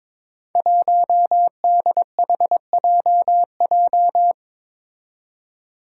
Morse code, heard as a keyed tone, about 22 wpm, 700 Hz. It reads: 1DHJJ
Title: Five Challenging Letter-Number Combinations 22wpm